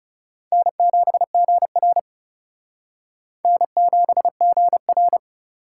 Morse code, heard as a keyed tone, about 34 wpm, 700 Hz. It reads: N7GR N7GR
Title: Speed Racing - USA 1x2 Call Signs 30wpm